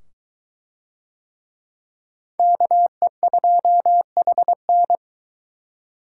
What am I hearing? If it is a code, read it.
Morse code, heard as a keyed tone, about 21 wpm, 700 Hz. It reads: KE2HN